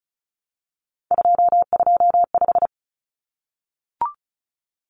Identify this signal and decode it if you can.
Morse code, heard as a keyed tone, about 34 wpm, 700 Hz. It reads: 225
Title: Common Cut Numbers - 3-Digit 35wpm